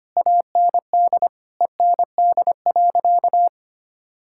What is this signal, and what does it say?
Morse code, heard as a keyed tone, about 25 wpm, 700 Hz. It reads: AND END.
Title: Sentences from Top 1000 Words - Review Part 6 - 25wpm